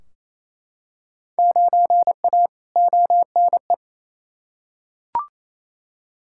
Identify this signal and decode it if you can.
Morse code, heard as a keyed tone, about 28 wpm, 700 Hz. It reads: 9A ONE
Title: Field Day Exchanges Encore - 28wpm